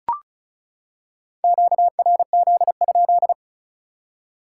Morse code, heard as a keyed tone, about 36 wpm, 700 Hz. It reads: QRZ?
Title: Top 180 QSO Elements 35wpm